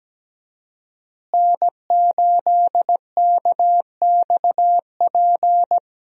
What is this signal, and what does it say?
Morse code, heard as a keyed tone, about 17 wpm, 700 Hz. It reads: N8KXP
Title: USA Call Signs - 5 Characters - Repeated 3x - 17wpm